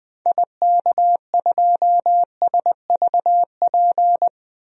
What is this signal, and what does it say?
Morse code, heard as a keyed tone, about 21 wpm, 700 Hz. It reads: IK2SVP
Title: DXCC-100 Call Signs - 6 Characters - 20wpm